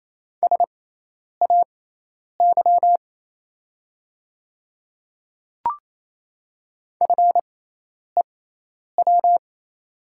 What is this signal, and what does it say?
Morse code, heard as a keyed tone, about 28 wpm, spaced as for 10 wpm, 690 Hz. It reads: SAY FEW
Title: 6x Character Spacing - Top 300 Words - 28wpm